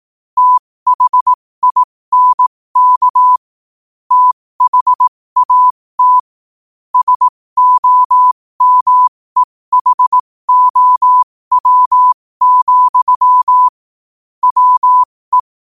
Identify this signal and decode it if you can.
Morse code, heard as a keyed tone, about 18 wpm, spaced as for 15 wpm, 1000 Hz. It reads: THINK THAT SOMEHOW, WE
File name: qotdfiles / 15wpm